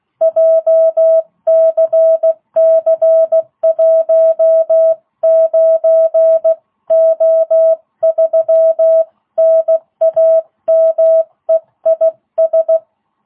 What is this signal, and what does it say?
Morse code, heard as a keyed tone, about 16 wpm, 660 Hz. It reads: JCC19O3 NAME IS